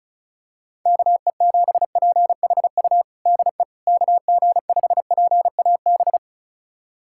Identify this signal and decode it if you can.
Morse code, heard as a keyed tone, about 34 wpm, 700 Hz. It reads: KE7PHU DE KG5PAB
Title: USA Callsign DE Callsign - 6 Characters - 35wpm